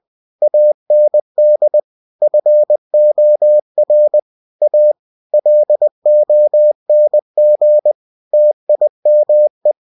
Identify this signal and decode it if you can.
Morse code, heard as a keyed tone, about 20 wpm, 600 Hz. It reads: AND FOR A LONG TIME
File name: cw-books / winnie-the-pooh / winnie-20wpm-page-0110